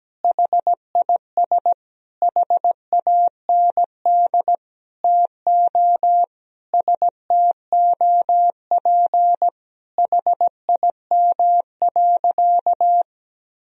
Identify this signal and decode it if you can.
Morse code, heard as a keyed tone, about 17 wpm, 700 Hz. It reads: HIS HAND TO STOP HIM.
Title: Sentences from Top 1000 Words - Review Part 4 - 17wpm